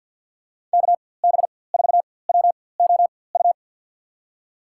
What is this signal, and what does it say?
Morse code, heard as a keyed tone, about 52 wpm, spaced as for 24 wpm, 700 Hz. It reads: KB4WOU
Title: USA Call Signs - 4x Character Spacing - 50wpm